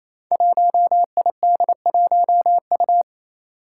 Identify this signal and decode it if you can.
Morse code, heard as a keyed tone, about 28 wpm, 700 Hz. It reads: 1ID1U
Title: Five Challenging Letter-Number Combinations 28wpm